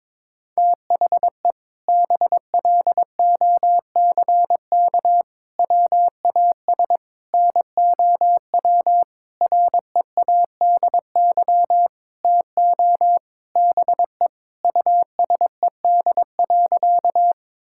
Morse code, heard as a keyed tone, about 22 wpm, 700 Hz. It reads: THE BLOCK WAS NOW READY TO BE USED.
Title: Sentences from Top 1600 Words 22wpm